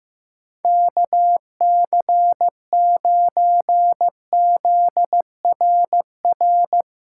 Morse code, heard as a keyed tone, about 16 wpm, 700 Hz. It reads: KC9ZRR
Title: USA Callsign DE Callsign 15wpm